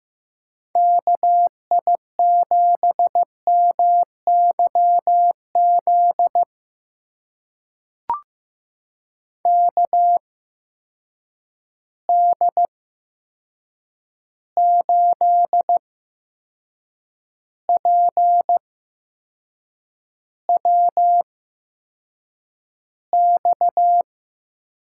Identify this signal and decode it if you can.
Morse code, heard as a keyed tone, about 15 wpm, spaced as for 6 wpm, 700 Hz. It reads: KI7MYZ KD8PWX
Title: USA Call Signs - 8x Character Spacing - 15wpm